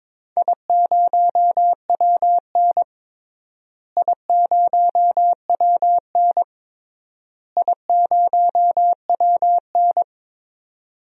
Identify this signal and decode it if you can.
Morse code, heard as a keyed tone, about 22 wpm, 700 Hz. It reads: I0WN I0WN I0WN
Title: DXCC-100 Call Signs - 4 Characters - Repeated 3x - 22wpm